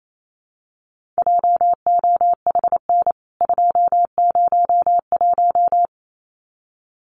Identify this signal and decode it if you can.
Morse code, heard as a keyed tone, about 28 wpm, 700 Hz. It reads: JOHN 201
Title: CWT Exchange 28wpm